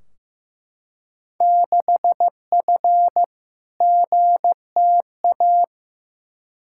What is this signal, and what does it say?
Morse code, heard as a keyed tone, about 15 wpm, 700 Hz. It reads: 6F GTA
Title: Field Day Exchanges Encore - 15wpm